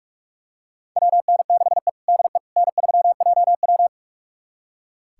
Morse code, heard as a keyed tone, about 46 wpm, 700 Hz. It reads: WN6E DE N3JW
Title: USA Callsign DE Callsign - 4 Characters - 45wpm